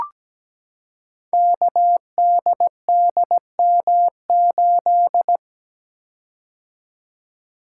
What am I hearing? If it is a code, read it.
Morse code, heard as a keyed tone, about 17 wpm, 700 Hz. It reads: KDDM8
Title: Code Groups - 5 Letter:Numbers - 17wpm